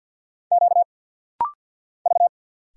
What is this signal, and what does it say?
Morse code, heard as a keyed tone, about 50 wpm, 700 Hz. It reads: Q V
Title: Single Letters - Rapid-Fire 50wpm